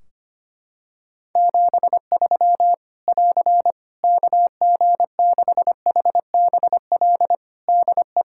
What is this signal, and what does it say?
Morse code, heard as a keyed tone, about 25 wpm, 700 Hz. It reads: 73 <AR> KG6HBL DE
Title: Practice QSO - Exchange 4 - Ending - 25wpm